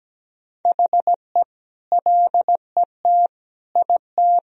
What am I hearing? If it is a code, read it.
Morse code, heard as a keyed tone, about 17 wpm, 700 Hz. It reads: HE LET IT
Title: Sentences from Top 1000 Words - Review Part 1 - 17wpm